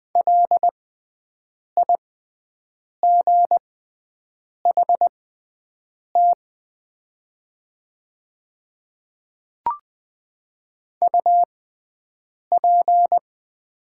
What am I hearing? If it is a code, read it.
Morse code, heard as a keyed tone, about 20 wpm, spaced as for 7 wpm, 700 Hz. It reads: LIGHT UP
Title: 6x Character Spacing - Top 200 Words - 20wpm